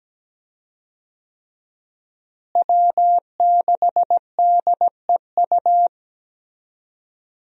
Morse code, heard as a keyed tone, about 17 wpm, 700 Hz. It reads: W6DEU
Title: USA Call Signs - 5 Characters - Repeated 3x - 17wpm